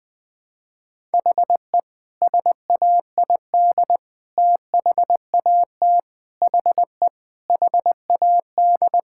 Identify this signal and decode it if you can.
Morse code, heard as a keyed tone, about 21 wpm, 700 Hz. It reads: HE SAID THAT HE HAD
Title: Sentences with 3-Word Phrases - 20wpm